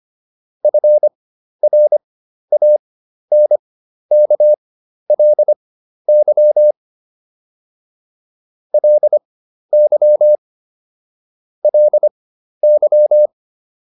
Morse code, heard as a keyed tone, about 25 wpm, spaced as for 12 wpm, 600 Hz. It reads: FRANKLY LY LY